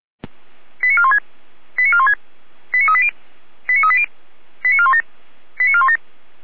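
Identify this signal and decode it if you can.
Five-tone selective call (ZVEI), heard as quick runs of five tones, tones of about 70 ms. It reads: 89416 89416 89380 89380 89417 89417
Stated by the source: ZVEI five-tone selective call received with an SDR.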